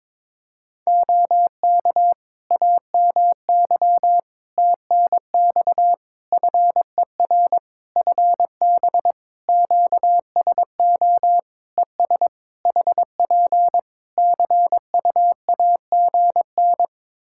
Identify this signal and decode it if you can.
Morse code, heard as a keyed tone, about 22 wpm, 700 Hz. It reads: OK AMY TNX FER FB QSO ES HP CUAGN
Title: Practice QSO - Exchange 4 - Ending - 22wpm